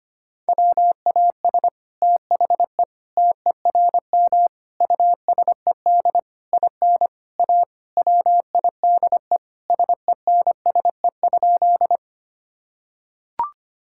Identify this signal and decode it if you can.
Morse code, heard as a keyed tone, about 25 wpm, 700 Hz. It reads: WAS THE TERM USED IN A WIDE SENSE?
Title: Sentences from Top 1000 Words - Review Part 6 - 25wpm